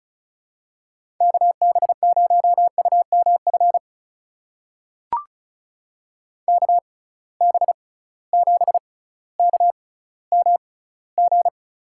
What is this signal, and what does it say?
Morse code, heard as a keyed tone, about 34 wpm, spaced as for 18 wpm, 700 Hz. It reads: KB0UMF KB7KMG
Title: USA Call Signs - 6 Characters - 6x Character Spacing - 35wpm